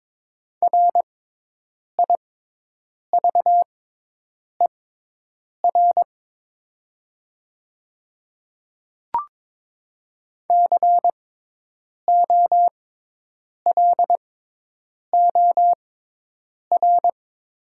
Morse code, heard as a keyed tone, about 22 wpm, spaced as for 8 wpm, 690 Hz. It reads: RIVER COLOR
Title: 6x Character Spacing - Top 500 Words - 22wpm